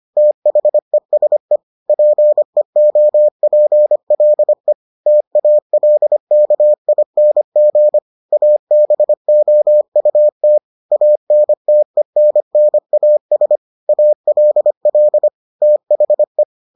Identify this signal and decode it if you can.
Morse code, heard as a keyed tone, about 25 wpm, 600 Hz. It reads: THESE PEOPLE TALKING ABOUT ANTENNAS ALL THE